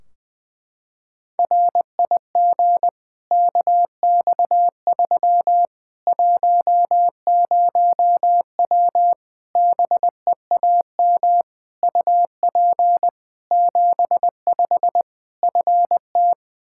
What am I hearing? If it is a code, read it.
Morse code, heard as a keyed tone, about 20 wpm, 700 Hz. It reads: RIG KX3 10W BEAM UP 75 FT
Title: Practice QSO - Exchange 2 - Info - 20wpm